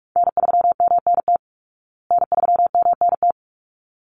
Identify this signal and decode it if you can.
Morse code, heard as a keyed tone, about 44 wpm, 700 Hz. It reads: N3MNT N3MNT
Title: Call Signs Repeated 3x - 45wpm